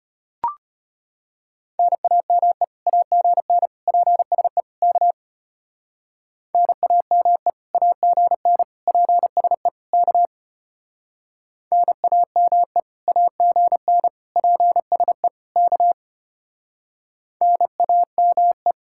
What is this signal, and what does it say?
Morse code, heard as a keyed tone, about 32 wpm, 700 Hz. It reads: NAME AGN PSE K NAME AGN PSE K NAME AGN PSE K NAME